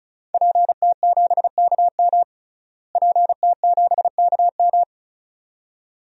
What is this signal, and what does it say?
Morse code, heard as a keyed tone, about 34 wpm, 700 Hz. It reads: PT7KM PT7KM
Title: DXCC-100 Call Signs - 5 Characters - Repeated 2x - 35wpm